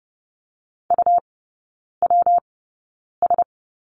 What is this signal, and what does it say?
Morse code, heard as a keyed tone, about 30 wpm, spaced as for 9 wpm, 700 Hz. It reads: UWS
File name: Lesson 19e - Letter w - Three Characters 30wpm